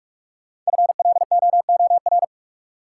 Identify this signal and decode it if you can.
Morse code, heard as a keyed tone, about 42 wpm, 700 Hz. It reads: FLOOR
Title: Sets of 3 Words - New Word Drill - 45wpm